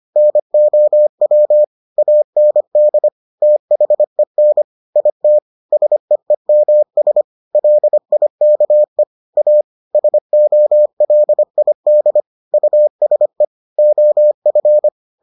Morse code, heard as a keyed tone, about 25 wpm, 600 Hz. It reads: NOW AND THEN IT SEEMS LIKE A SOLID USE OF